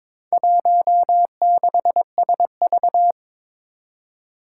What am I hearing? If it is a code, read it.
Morse code, heard as a keyed tone, about 22 wpm, 700 Hz. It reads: 16SV
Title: Four Challenging Letter-Number Combinations 22wpm